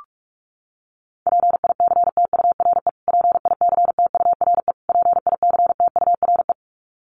Morse code, heard as a keyed tone, about 44 wpm, 700 Hz. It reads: PICTURE PICTURE PICTURE
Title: Top 300 Words Repeated 45wpm